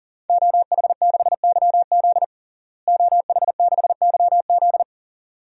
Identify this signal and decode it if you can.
Morse code, heard as a keyed tone, about 40 wpm, 700 Hz. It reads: OH6YZ OH6YZ
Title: DXCC-100 Call Signs - 5 Characters - Repeated 2x - 40wpm